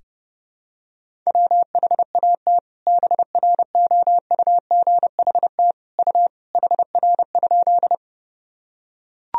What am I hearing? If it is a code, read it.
Morse code, heard as a keyed tone, about 30 wpm, 700 Hz. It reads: WHAT BROUGHT U HR?
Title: Sentences from Top 500 Words and Abbreviations 30wpm